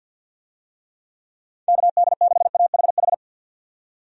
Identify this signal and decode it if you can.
Morse code, heard as a keyed tone, about 51 wpm, 700 Hz. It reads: KD6AHH
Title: USA Call Signs - 6 Characters - 50wpm